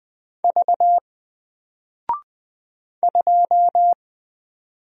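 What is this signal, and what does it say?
Morse code, heard as a keyed tone, about 20 wpm, 700 Hz. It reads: V 2
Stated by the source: Single letter and number Morse drill at 20 wpm, from N4LCD.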